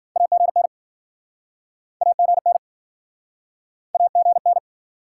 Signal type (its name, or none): Morse code